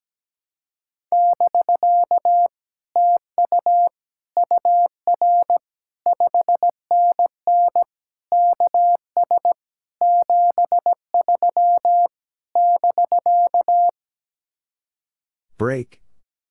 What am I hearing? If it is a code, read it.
Morse code, heard as a keyed tone, about 17 wpm, 700 Hz. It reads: <BK> TU UR 5NN KS 73 <BK>
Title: Speed-Racing Common QSO Phrases - 17wpm